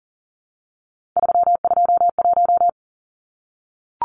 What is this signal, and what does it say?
Morse code, heard as a keyed tone, about 40 wpm, 700 Hz. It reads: 321
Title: Numbers - 3-Digit 40wpm